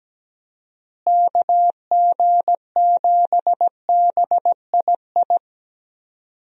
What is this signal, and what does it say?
Morse code, heard as a keyed tone, about 17 wpm, 700 Hz. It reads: KG7BII
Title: USA Call Signs - 6 Characters - Repeated 2x - 17wpm